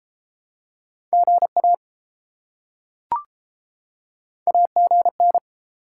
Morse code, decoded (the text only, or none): GA AGN